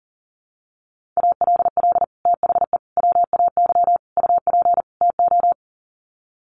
Morse code, heard as a keyed tone, about 40 wpm, 700 Hz. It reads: ALL THE WAY UP TO
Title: Sets of 5 Words Encore 40wpm